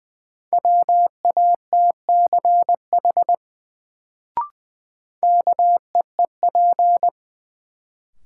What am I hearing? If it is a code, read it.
Morse code, heard as a keyed tone, about 20 wpm, 700 Hz. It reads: WATCH KEEP
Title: Sentences from Top 300 Words - New Word Drill - 20wpm